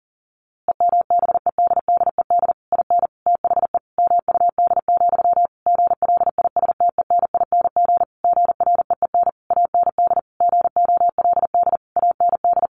Morse code, heard as a keyed tone, about 40 wpm, 700 Hz. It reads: EMBEDDED IN THE MUD, GLISTENING GREEN AND GOLD AND
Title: A Sound Of Thunder 40wpm